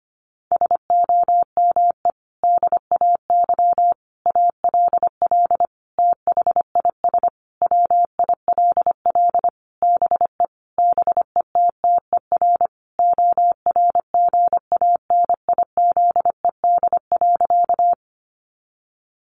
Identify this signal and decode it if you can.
Morse code, heard as a keyed tone, about 25 wpm, 700 Hz. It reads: SOME DAY ALL THIS WILL BE BETTER ORGANIZED.
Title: Sentences from Top 1300 Words 25wpm